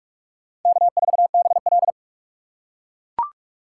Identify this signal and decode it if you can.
Morse code, heard as a keyed tone, about 41 wpm, 700 Hz. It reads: K4BL